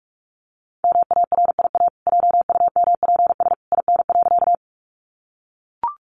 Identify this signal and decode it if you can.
Morse code, heard as a keyed tone, about 44 wpm, 700 Hz. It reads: MARIA JUMPS IN.
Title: Easy Phrases 45wpm